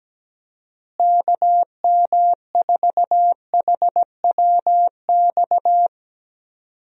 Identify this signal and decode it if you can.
Morse code, heard as a keyed tone, about 17 wpm, 700 Hz. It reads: KM4HWX
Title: USA Call Signs - 6 Characters - Repeated 2x - 17wpm